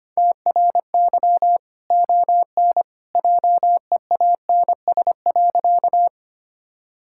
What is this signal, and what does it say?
Morse code, heard as a keyed tone, about 25 wpm, 700 Hz. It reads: TRY ON JEANS.
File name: Easy Sentences with Hints - 25wpm